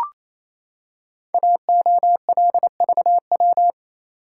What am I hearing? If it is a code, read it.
Morse code, heard as a keyed tone, about 28 wpm, 700 Hz. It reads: AOLVW
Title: Code Groups - 5 Letters - 28wpm